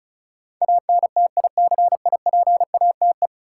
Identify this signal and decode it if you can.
Morse code, heard as a keyed tone, about 36 wpm, 700 Hz. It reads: ANTICIPATE
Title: Sentences from Top 1500 Words - New Word Drill - 35wpm